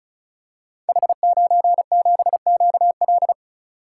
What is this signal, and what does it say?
Morse code, heard as a keyed tone, about 36 wpm, 700 Hz. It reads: H97QL
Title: Code Groups - 5 Letter:Numbers - 35wpm